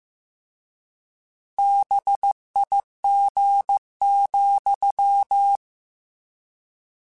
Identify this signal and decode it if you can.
Morse code, heard as a keyed tone, about 15 wpm, 800 Hz. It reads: BIG,